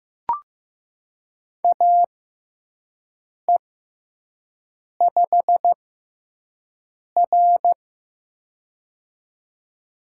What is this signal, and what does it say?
Morse code, heard as a keyed tone, about 15 wpm, spaced as for 5 wpm, 700 Hz. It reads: AE5R
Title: USA Call Signs - 6x Character Spacing - 15wpm